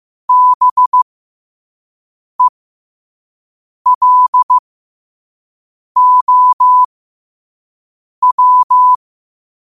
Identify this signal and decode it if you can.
Morse code, heard as a keyed tone, about 15 wpm, spaced as for 5 wpm, 1000 Hz. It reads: BELOW
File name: qotdfiles / 5wpm